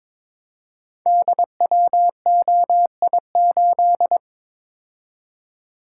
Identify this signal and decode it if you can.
Morse code, heard as a keyed tone, about 22 wpm, 700 Hz. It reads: DWOI8
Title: Code Groups - 5 Letter:Numbers - 22wpm